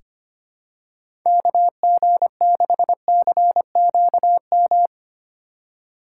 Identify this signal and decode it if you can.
Morse code, heard as a keyed tone, about 25 wpm, 700 Hz. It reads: KG6CQM
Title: USA Call Signs - 6 Characters - 3x Character Spacing - 25wpm